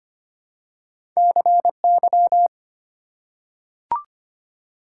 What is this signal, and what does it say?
Morse code, heard as a keyed tone, about 25 wpm, 700 Hz. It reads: CY
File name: Two Letter-Number Combinations - 25wpm